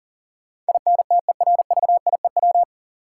Morse code, heard as a keyed tone, about 40 wpm, 700 Hz. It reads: INTERVIEW
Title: Sentences from Top 1600 Words - New Word Drill - 40wpm